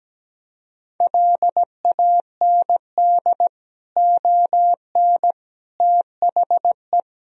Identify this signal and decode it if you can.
Morse code, heard as a keyed tone, about 17 wpm, 700 Hz. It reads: LAND ON THE
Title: Sentences from Top 1000 Words - Review Part 1 - 17wpm